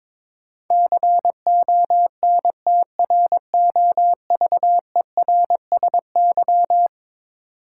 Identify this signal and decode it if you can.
Morse code, heard as a keyed tone, about 21 wpm, 700 Hz. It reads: CONTROVERSY
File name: practice / 22wpm / Eleven Letter Words - 22wpm